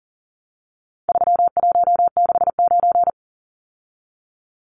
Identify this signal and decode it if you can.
Morse code, heard as a keyed tone, about 40 wpm, 700 Hz. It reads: 3169
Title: Numbers - 4-Digit 40wpm